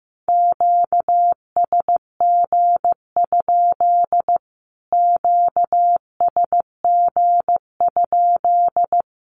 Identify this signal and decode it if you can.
Morse code, heard as a keyed tone, about 15 wpm, 700 Hz. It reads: QSG? QSG?
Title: Q-Codes Repeated 3x 15wpm